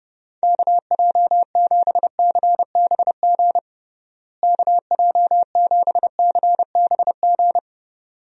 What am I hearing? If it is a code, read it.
Morse code, heard as a keyed tone, about 32 wpm, 700 Hz. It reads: KJ7CBG KJ7CBG
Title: USA Call Signs - 6 Characters - Repeated 2x - 30wpm